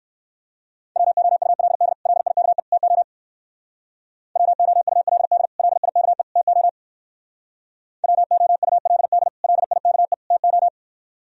Morse code, heard as a keyed tone, about 59 wpm, 700 Hz. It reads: WOULD LIKE TO WOULD LIKE TO WOULD LIKE TO